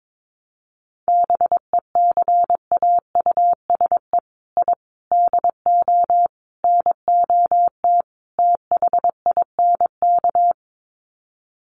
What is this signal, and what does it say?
Morse code, heard as a keyed tone, about 22 wpm, 700 Hz. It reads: BECAUSE I DO NOT THINK
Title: Sets of 5 Words Encore 22wpm